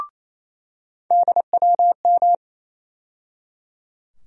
Morse code, heard as a keyed tone, about 28 wpm, 700 Hz. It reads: DWM
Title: USA Call Sign Postfixes - 3 Letters 28wpm